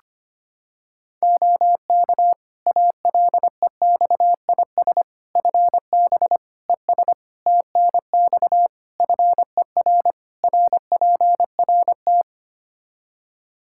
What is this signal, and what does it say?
Morse code, heard as a keyed tone, about 25 wpm, 700 Hz. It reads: OK ALEXIS FB ES TNX FER RPRT
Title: Practice QSO - Exchange 2 - Info - 25wpm